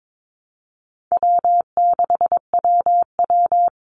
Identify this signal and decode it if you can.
Morse code, heard as a keyed tone, about 22 wpm, 700 Hz. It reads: W6WW
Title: NA SOTA Chaser Callsigns 22wpm